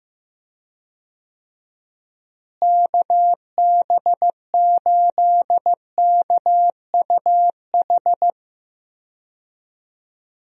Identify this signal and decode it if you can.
Morse code, heard as a keyed tone, about 15 wpm, 700 Hz. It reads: KB8KUH
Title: USA Call Signs - 6 Characters - Repeated 2x - 15wpm